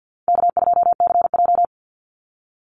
Morse code, heard as a keyed tone, about 52 wpm, 700 Hz. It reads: K2CJ
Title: Easy Call Signs - 50wpm